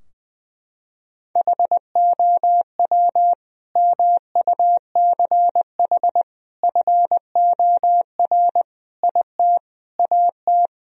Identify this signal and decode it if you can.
Morse code, heard as a keyed tone, about 20 wpm, 700 Hz. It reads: HOW MUCH FOR IT AT
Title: Sentences from Top 1000 Words - Review Part 3 - 20wpm